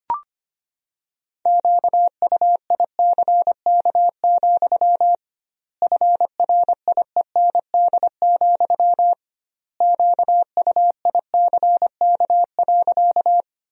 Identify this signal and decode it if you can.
Morse code, heard as a keyed tone, about 25 wpm, 700 Hz. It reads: QUICK, FRIEND, QUICK.
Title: 2x Word-Spacing - Sentences from Top 500 Words - 25wpm